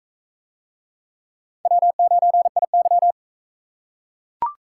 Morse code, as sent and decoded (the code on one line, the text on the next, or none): .-- ----. .. -.--
W9IY